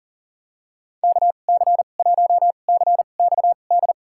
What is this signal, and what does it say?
Morse code, heard as a keyed tone, about 38 wpm, spaced as for 28 wpm, 700 Hz. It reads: KC1CXD
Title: USA Call Signs - 6 Characters - 2x Character Spacing - 40wpm